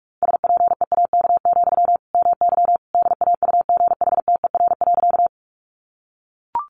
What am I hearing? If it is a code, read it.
Morse code, heard as a keyed tone, about 44 wpm, 700 Hz. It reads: SPEAK, MY DAUGHTER.
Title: Sentences from Top 1500 Words 45wpm